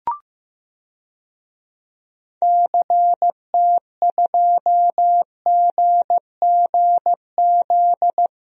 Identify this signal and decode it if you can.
Morse code, heard as a keyed tone, about 15 wpm, 700 Hz. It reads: CT2GGZ
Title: DXCC-100 Call Signs - 6 Characters - 15wpm